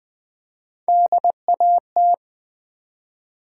Morse code, heard as a keyed tone, about 20 wpm, 700 Hz. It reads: DAT
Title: Build A Word - 1 Letter At A Time - 4-Letter Words - 20wpm